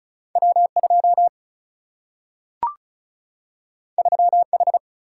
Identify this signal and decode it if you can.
Morse code, heard as a keyed tone, about 36 wpm, 700 Hz. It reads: W 2 3 H
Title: Single-Letter Single-Number 35wpm